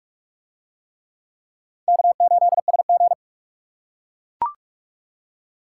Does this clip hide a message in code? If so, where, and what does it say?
Yes: Morse code from 1.9 to 3.1 s, heard as a keyed tone, about 41 wpm, 700 Hz. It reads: K8SG